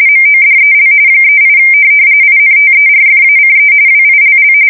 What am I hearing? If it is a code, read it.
RTTY, about 45.45 baud. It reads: MODEL 38 INTERFACE WILL
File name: fsk170 / teletype-corp